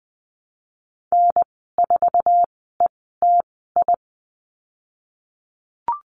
Morse code, heard as a keyed tone, about 20 wpm, spaced as for 15 wpm, 700 Hz. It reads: N4ETI
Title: 048 Lesson 9g - Number 4 - Call Signs 20wpm